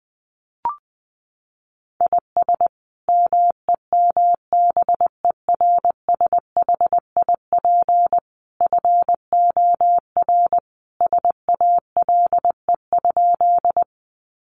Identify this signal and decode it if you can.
Morse code, heard as a keyed tone, about 20 wpm, 700 Hz. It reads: IS MEMBERSHIP FOR SALE?
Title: Sentences from Top 1400 Words 20wpm